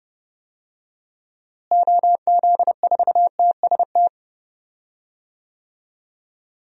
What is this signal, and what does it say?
Morse code, heard as a keyed tone, about 30 wpm, 700 Hz. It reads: OZ4TST
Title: DXCC-100 Call Signs - 6 Characters - Repeated 3x - 30wpm